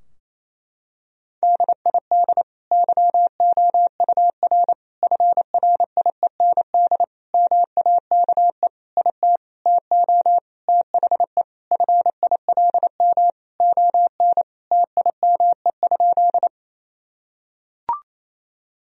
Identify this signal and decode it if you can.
Morse code, heard as a keyed tone, about 28 wpm, 700 Hz. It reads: DID YOUR FRIEND MAKE IT TO THE FILM ON TIME?
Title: Sentences from Top 1000 Words - Review Part 3 - 28wpm